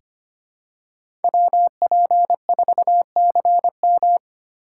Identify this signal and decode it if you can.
Morse code, heard as a keyed tone, about 25 wpm, 700 Hz. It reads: WP4CM
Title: USA Call Signs - 3x Character Spacing - 25wpm